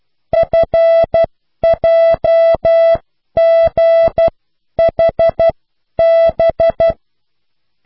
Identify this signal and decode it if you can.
Morse code, heard as a keyed tone, about 12 wpm, spaced as for 10 wpm, 660 Hz. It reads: FJGHB